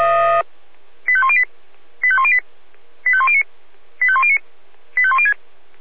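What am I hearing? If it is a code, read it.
Five-tone selective call (ZVEI), heard as quick runs of five tones, tones of about 70 ms. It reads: 85108 85108 85109 85109 85106
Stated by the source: ZVEI five-tone selective call received on an SDR.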